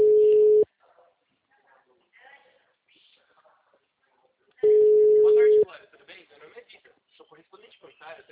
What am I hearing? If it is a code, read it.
Ringback tone (Europe, 425 Hz), with speech.
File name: out- -2013- - - .53967